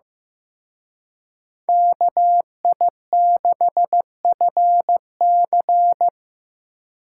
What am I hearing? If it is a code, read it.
Morse code, heard as a keyed tone, about 18 wpm, 700 Hz. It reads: KI6FC
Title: Speed Racing - USA 2x2 Call Signs 15wpm